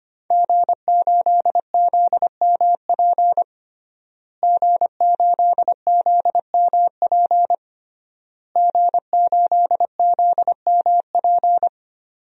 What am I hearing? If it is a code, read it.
Morse code, heard as a keyed tone, about 25 wpm, 700 Hz. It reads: G8ZMP G8ZMP G8ZMP
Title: DXCC-100 Call Signs - 5 Characters - Repeated 3x - 25wpm